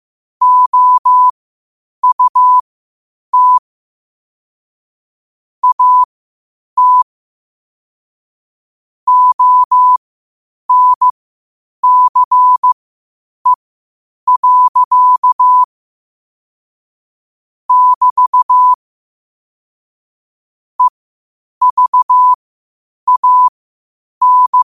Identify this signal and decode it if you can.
Morse code, heard as a keyed tone, about 15 wpm, spaced as for 8 wpm, 1000 Hz. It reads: OUT AT ONCE. <BT> EVAN
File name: qotdfiles / 8wpm